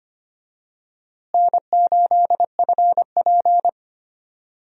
Morse code, heard as a keyed tone, about 25 wpm, 700 Hz. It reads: N8FP
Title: USA Call Signs - 4 Characters - Repeated 3x - 25wpm